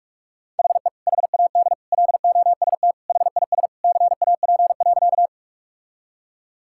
Morse code, heard as a keyed tone, about 41 wpm, 700 Hz. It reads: HE HAD LOST HIS CAP.
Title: Sentences from Top 1600 Words 45wpm